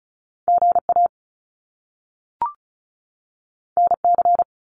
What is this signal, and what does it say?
Morse code, heard as a keyed tone, about 36 wpm, 700 Hz. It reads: GA NC
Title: US States 35wpm